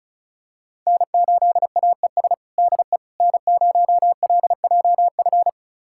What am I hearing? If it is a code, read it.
Morse code, heard as a keyed tone, about 36 wpm, 700 Hz. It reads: N8AES DE N0LJF
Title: USA Callsign DE Callsign - 5 Characters - 35wpm